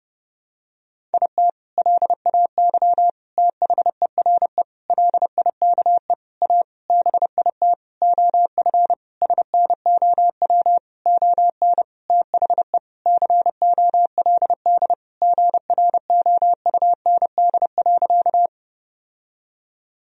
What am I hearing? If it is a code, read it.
Morse code, heard as a keyed tone, about 30 wpm, 700 Hz. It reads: IT LAY THERE LIKE A BIT OF SNOW ON THE COLD GROUND.
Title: Sentences from Top 1000 Words - Review Part 3 - 30wpm